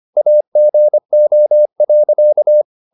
Morse code, heard as a keyed tone, about 25 wpm, 600 Hz. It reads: AGO.